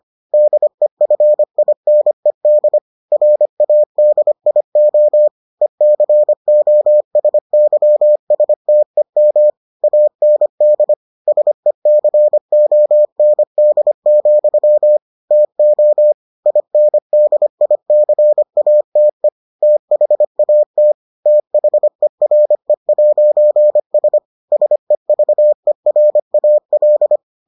Morse code, heard as a keyed tone, about 25 wpm, 600 Hz. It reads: DEFINED RADIO ECOSYSTEM AND SECOND, TO INDICATE THAT THERE'S SEVERAL